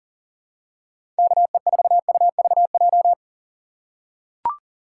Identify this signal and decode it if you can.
Morse code, heard as a keyed tone, about 40 wpm, 700 Hz. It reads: KE4UVJ
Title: USA Call Signs - 6 Characters - 4x Character Spacing - 40wpm